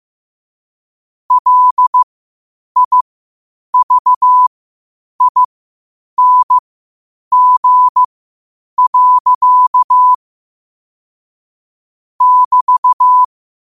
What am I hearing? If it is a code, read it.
Morse code, heard as a keyed tone, about 15 wpm, spaced as for 8 wpm, 1000 Hz. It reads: LIVING. <BT>